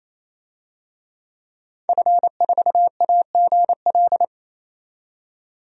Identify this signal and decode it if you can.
Morse code, heard as a keyed tone, about 28 wpm, 700 Hz. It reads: F4AGL
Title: DXCC-100 Call Signs - 5 Characters - 28wpm